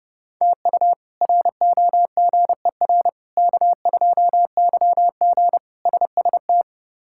Morse code, heard as a keyed tone, about 31 wpm, 700 Hz. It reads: TU ROGER K2YG SST
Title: K1USN SST QSO Practice - 30wpm